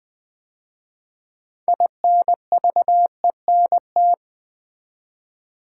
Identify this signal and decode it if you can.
Morse code, heard as a keyed tone, about 20 wpm, 700 Hz. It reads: INVENT
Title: 6 Letter Words - First 3 Letters Given - 20wpm